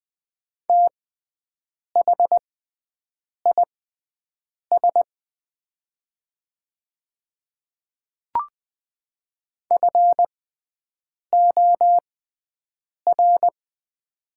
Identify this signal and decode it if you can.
Morse code, heard as a keyed tone, about 20 wpm, spaced as for 7 wpm, 700 Hz. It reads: THIS FOR
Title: 6x Character Spacing - Top 100 Words - 20wpm